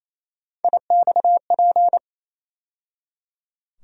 Morse code, heard as a keyed tone, about 28 wpm, 700 Hz. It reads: IXP
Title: USA Call Sign Postfixes - 3 Letters 28wpm